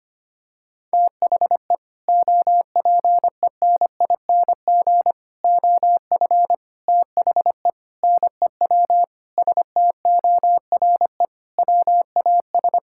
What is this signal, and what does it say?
Morse code, heard as a keyed tone, about 25 wpm, 700 Hz. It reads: THE OPENING OF THE NEW STORE WAS
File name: practice / 25wpm / Sentences with 3-Word Phrases - 25wpm